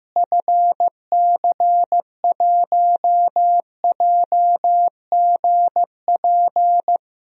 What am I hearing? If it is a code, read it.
Morse code, heard as a keyed tone, about 15 wpm, 700 Hz. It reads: FC1JGP
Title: DXCC-100 Call Signs - 6 Characters - Repeated 3x - 15wpm